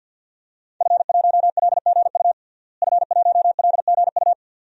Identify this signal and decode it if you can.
Morse code, heard as a keyed tone, about 52 wpm, 700 Hz. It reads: F1LGU F1LGU
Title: DXCC-100 Call Signs - 5 Characters - Repeated 2x - 50wpm